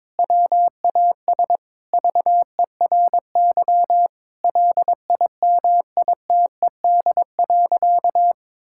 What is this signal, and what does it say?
Morse code, heard as a keyed tone, about 21 wpm, 700 Hz. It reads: WAS VERY LIMITED.
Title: Sentences from Top 1000 Words - Review Part 6 - 22wpm